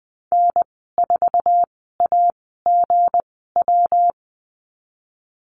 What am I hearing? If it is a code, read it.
Morse code, heard as a keyed tone, about 20 wpm, spaced as for 15 wpm, 700 Hz. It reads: N4AGW
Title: 167 Lesson 27g - Number 7 - Call Signs 20wpm